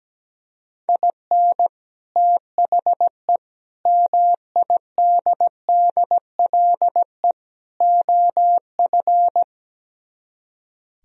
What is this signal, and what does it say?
Morse code, heard as a keyed tone, about 17 wpm, 700 Hz. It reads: IN THE MIDDLE OF
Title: Speed-Racing - Sets of 4 Words 17wpm